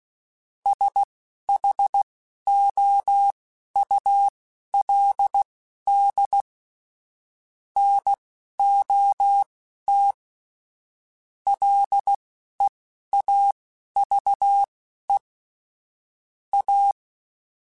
Morse code, heard as a keyed tone, about 16 wpm, spaced as for 11 wpm, 800 Hz. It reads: SHOULD NOT LEAVE A